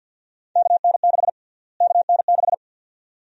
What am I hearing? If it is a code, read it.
Morse code, heard as a keyed tone, about 47 wpm, 700 Hz. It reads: KN6 KN6
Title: USA Call Sign Prefixes - Repeated 2x - 50wpm